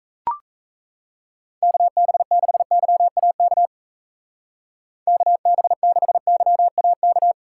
Morse code, heard as a keyed tone, about 35 wpm, 700 Hz. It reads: KB6YAK KB6YAK
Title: Speed Racing - USA Call Signs 28wpm